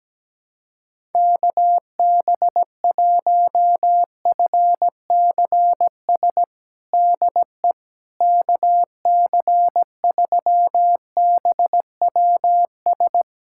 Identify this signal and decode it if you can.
Morse code, heard as a keyed tone, about 17 wpm, 700 Hz. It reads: KB1FCS DE KC3BWS
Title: USA Callsign DE Callsign 17wpm